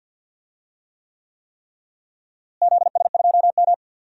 Morse code, heard as a keyed tone, about 53 wpm, 700 Hz. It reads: ZS2M
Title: DXCC-100 Call Signs - 4 Characters - Repeated 3x - 50wpm